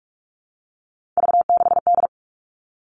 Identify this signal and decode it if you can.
Morse code, heard as a keyed tone, about 44 wpm, 700 Hz. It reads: V6D
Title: All Cut Numbers - 3-Digit 45wpm